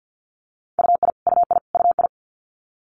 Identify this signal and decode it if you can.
Morse code, heard as a keyed tone, about 50 wpm, 700 Hz. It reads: UI UI UI
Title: Multi-letter Phonemes Repeated 3x 50wpm